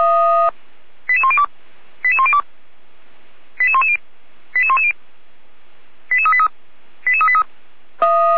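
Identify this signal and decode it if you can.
Five-tone selective call (ZVEI), heard as quick runs of five tones, tones of about 70 ms. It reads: 88102 88102 88110 88110 88383 88383
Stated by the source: ZVEI five-tone selective call received by SDR, from the Feuerwehr Orlamünde (fire brigade) audio server.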